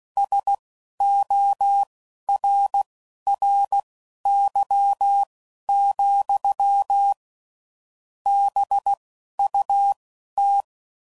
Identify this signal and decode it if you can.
Morse code, heard as a keyed tone, about 16 wpm, spaced as for 11 wpm, 800 Hz. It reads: SORRY, BUT
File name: CW Code Groups @10WPM / QSOs / QSO kc7sac